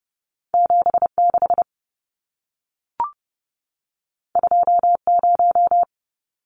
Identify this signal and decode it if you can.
Morse code, heard as a keyed tone, about 30 wpm, 700 Hz. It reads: 76 20
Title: Numbers - 2-Digit 30wpm